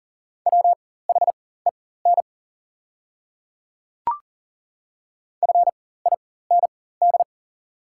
Morse code, heard as a keyed tone, about 40 wpm, spaced as for 19 wpm, 690 Hz. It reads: WHEN FIND
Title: 4x Character Spacing - Top 100 Words - 40wpm